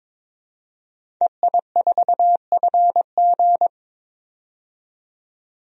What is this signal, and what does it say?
Morse code, heard as a keyed tone, about 22 wpm, 700 Hz. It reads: EI4FG
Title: DXCC-100 Call Signs - 5 Characters - 22wpm